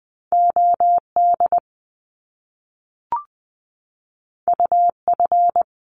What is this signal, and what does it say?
Morse code, heard as a keyed tone, about 20 wpm, 700 Hz. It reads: OD UF
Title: Top Two Letter Combinations in Words 20wpm